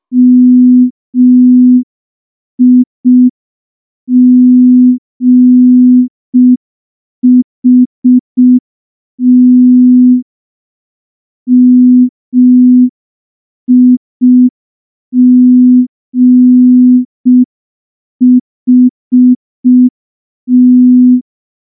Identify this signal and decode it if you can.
Morse code, heard as a keyed tone, about 5 wpm, 250 Hz. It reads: MIGHT MIGHT
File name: morse / morse-training-05WPM